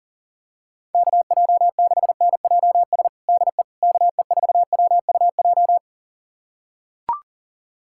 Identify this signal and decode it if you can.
Morse code, heard as a keyed tone, about 37 wpm, 700 Hz. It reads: KJ6NJS DE KE4WUJ
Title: USA Callsign DE Callsign 40wpm